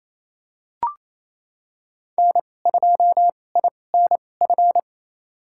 Morse code, heard as a keyed tone, about 28 wpm, spaced as for 20 wpm, 700 Hz. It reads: N2INF
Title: USA Call Signs - 5 Characters - 2x Character Spacing - 28wpm